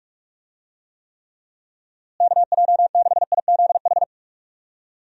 Morse code, heard as a keyed tone, about 42 wpm, 700 Hz. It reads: KJ6IZH
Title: USA Call Signs - 6 Characters - Repeated 2x - 45wpm